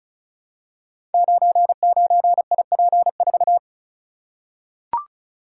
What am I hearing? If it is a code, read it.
Morse code, heard as a keyed tone, about 36 wpm, 700 Hz. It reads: 99IP4
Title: Code Groups - 5 Letter:Numbers - 35wpm